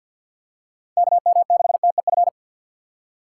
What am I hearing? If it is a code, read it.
Morse code, heard as a keyed tone, about 53 wpm, 700 Hz. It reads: KM6TEF
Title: USA Call Signs - 6 Characters - 4x Character Spacing - 50wpm